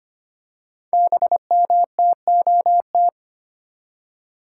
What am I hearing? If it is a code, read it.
Morse code, heard as a keyed tone, about 25 wpm, 700 Hz. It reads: BMTOT
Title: Code Groups - 5 Letters - 25wpm